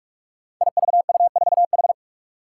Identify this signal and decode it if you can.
Morse code, heard as a keyed tone, about 43 wpm, 700 Hz. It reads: IVU4H
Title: Five Challenging Letter-Number Combinations 45wpm